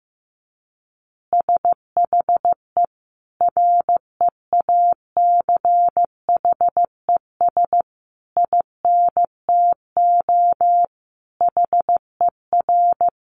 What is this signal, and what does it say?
Morse code, heard as a keyed tone, about 15 wpm, 700 Hz. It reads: SHE REACHES INTO HER
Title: Additional Easy Phrases 15wpm